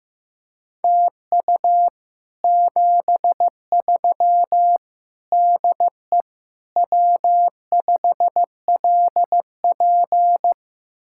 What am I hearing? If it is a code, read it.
Morse code, heard as a keyed tone, about 15 wpm, 700 Hz. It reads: TU 73 DE W5LP
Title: Speed-Racing Common QSO Phrases Encore - 15wpm